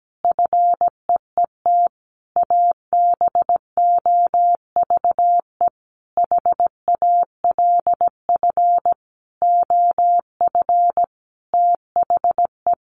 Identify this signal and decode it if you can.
Morse code, heard as a keyed tone, about 17 wpm, 700 Hz. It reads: FEET ABOVE HALF OF THE
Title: A Sound Of Thunder 17wpm